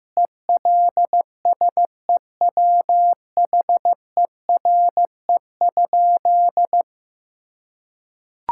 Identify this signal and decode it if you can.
Morse code, heard as a keyed tone, about 15 wpm, 700 Hz. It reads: ELSEWHERE?
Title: Sentences from Top 1600 Words 15wpm